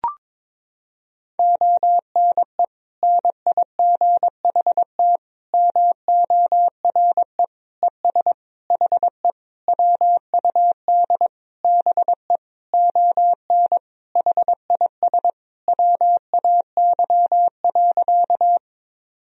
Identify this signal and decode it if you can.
Morse code, heard as a keyed tone, about 22 wpm, 700 Hz. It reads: ONE NIGHT MORE ES HE WUD BE ON HIS WAY.
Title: Sentences from Top 300 Words and Abbreviations 22wpm